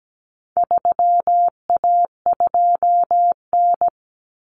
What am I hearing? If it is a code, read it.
Morse code, heard as a keyed tone, about 17 wpm, 700 Hz. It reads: 3A2N
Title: Common Cut Numbers - 4-Digit 17wpm